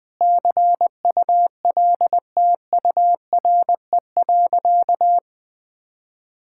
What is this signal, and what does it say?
Morse code, heard as a keyed tone, about 20 wpm, 700 Hz. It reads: CULTURE.
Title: Sentences from Top 1000 Words - Review Part 6 - 20wpm